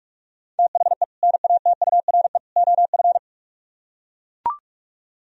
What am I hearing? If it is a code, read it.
Morse code, heard as a keyed tone, about 44 wpm, 700 Hz. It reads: THE NATURE OF